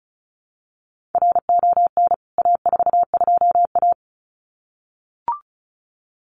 Morse code, heard as a keyed tone, about 34 wpm, 700 Hz. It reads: RON A42A
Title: CWT Exchange 35wpm